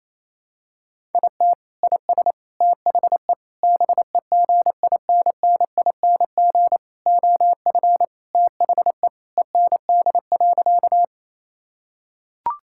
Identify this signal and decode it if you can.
Morse code, heard as a keyed tone, about 28 wpm, 700 Hz. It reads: IT IS THE BEGINNING OF THE END.
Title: Sentences from Top 1000 Words - Review Part 5 - 28wpm